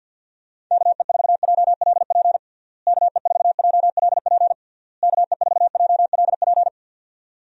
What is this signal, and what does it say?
Morse code, heard as a keyed tone, about 52 wpm, 700 Hz. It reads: KE4JLP KE4JLP KE4JLP
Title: USA Call Signs - 6 Characters - Repeated 3x - 50wpm